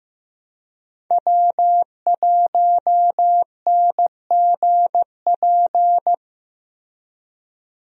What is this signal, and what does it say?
Morse code, heard as a keyed tone, about 15 wpm, 700 Hz. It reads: W1NGP
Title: USA Call Signs - 5 Characters - Repeated 3x - 15wpm